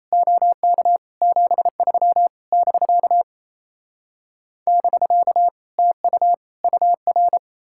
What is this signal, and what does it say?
Morse code, heard as a keyed tone, about 37 wpm, 700 Hz. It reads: OK 73 <BK> <BK> TU UR